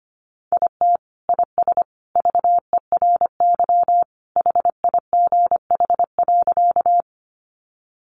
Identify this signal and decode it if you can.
Morse code, heard as a keyed tone, about 25 wpm, 700 Hz. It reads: IT IS VERY HIGH.
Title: Sentences from Top 200 Words 25wpm